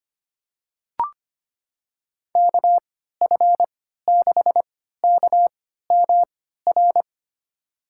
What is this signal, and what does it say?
Morse code, heard as a keyed tone, about 25 wpm, spaced as for 14 wpm, 700 Hz. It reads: KF6KMR
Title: USA Call Signs - 6 Characters - 3x Character Spacing - 25wpm